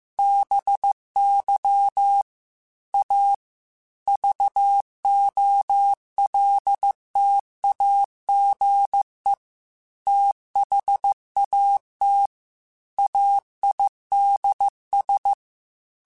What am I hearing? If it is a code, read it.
Morse code, heard as a keyed tone, about 15 wpm, 800 Hz. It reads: BY A VOLTAGE THAT AIDS
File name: Technical Subjects / Technical Subjects-mp3s / XL @13